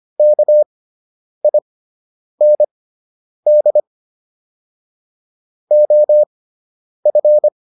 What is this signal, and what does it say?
Morse code, heard as a keyed tone, about 25 wpm, spaced as for 9 wpm, 600 Hz. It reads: KIND OF